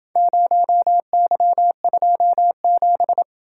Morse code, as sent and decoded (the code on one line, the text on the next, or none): ----- -.-- ..--- --...
0Y27